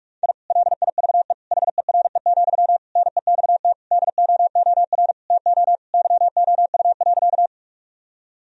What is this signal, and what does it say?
Morse code, heard as a keyed tone, about 46 wpm, 700 Hz. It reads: I LIVE HERE, NEXT DOOR TO YOU.
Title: Sentences from Top 1000 Words - Review Part 1 - 45wpm